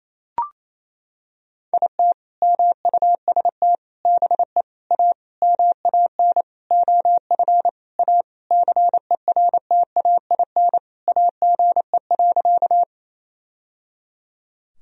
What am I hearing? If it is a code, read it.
Morse code, heard as a keyed tone, about 28 wpm, 700 Hz. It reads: IT MUST BE A MAN OF A CERTAIN AGE.
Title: Sentences from Top 1000 Words - Review Part 3 - 28wpm